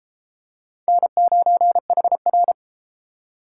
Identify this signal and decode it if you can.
Morse code, heard as a keyed tone, about 32 wpm, 700 Hz. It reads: N9HR